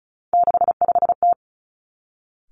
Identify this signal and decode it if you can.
Morse code, heard as a keyed tone, about 36 wpm, 700 Hz. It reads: 65T
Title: Three Challenging Letter-Number Combinations 35wpm